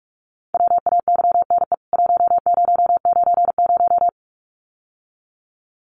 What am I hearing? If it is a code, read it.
Morse code, heard as a keyed tone, about 44 wpm, 700 Hz. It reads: WAYNE 1090
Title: CWT Exchange 45wpm